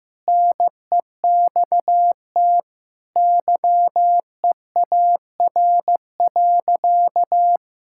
Morse code, heard as a keyed tone, about 15 wpm, 700 Hz. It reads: NEXT YEAR.
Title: Sentences from Top 1000 Words - Review Part 6 - 15wpm